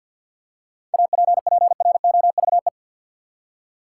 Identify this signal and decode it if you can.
Morse code, heard as a keyed tone, about 47 wpm, 700 Hz. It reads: APPROVE
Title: Seven Letter Words - 50wpm